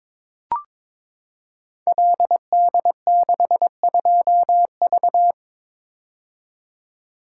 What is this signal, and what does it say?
Morse code, heard as a keyed tone, about 23 wpm, 700 Hz. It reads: LD62V
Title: Code Groups - 5 Letter:Numbers - 22wpm